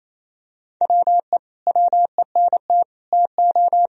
Morse code, heard as a keyed tone, about 28 wpm, 700 Hz. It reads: WE WENT TO
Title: Sentences from Top 1000 Words - Review Part 3 - 28wpm